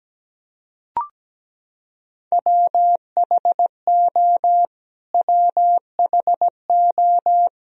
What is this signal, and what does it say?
Morse code, heard as a keyed tone, about 17 wpm, 700 Hz. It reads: WHO WHO
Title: Top 100 Words Repeated 17wpm